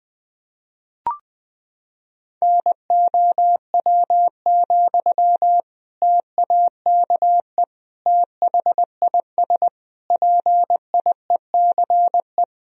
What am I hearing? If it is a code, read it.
Morse code, heard as a keyed tone, about 20 wpm, 700 Hz. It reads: NOW, TAKE THIS PIECE
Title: Sentences from Top 400 Words and Abbreviations 20wpm